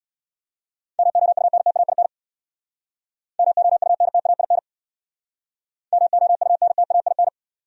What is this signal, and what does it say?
Morse code, heard as a keyed tone, about 67 wpm, 700 Hz. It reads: MOUNTAIN MOUNTAIN MOUNTAIN